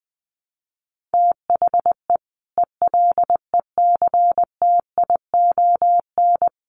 Morse code, heard as a keyed tone, about 19 wpm, 700 Hz. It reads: THE ELECTION
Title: Sentences from Top 1500 Words 20wpm